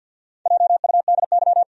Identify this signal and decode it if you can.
Morse code, heard as a keyed tone, about 52 wpm, 700 Hz. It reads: JUDY
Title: USA Common Names - Max 4 Letters - 50wpm